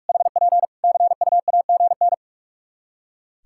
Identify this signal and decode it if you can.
Morse code, heard as a keyed tone, about 46 wpm, 700 Hz. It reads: HP CUAGN
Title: Practice QSO - Exchange 4 - Ending - 45wpm